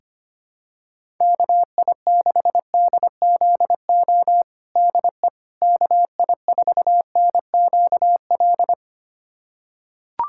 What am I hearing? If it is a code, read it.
Morse code, heard as a keyed tone, about 25 wpm, 700 Hz. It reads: KI6DZO DE KI4NQL
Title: USA Callsign DE Callsign 25wpm